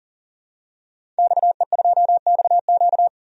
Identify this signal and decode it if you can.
Morse code, heard as a keyed tone, about 40 wpm, 700 Hz. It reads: XE2XQ
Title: DXCC-100 Call Signs - 5 Characters - 40wpm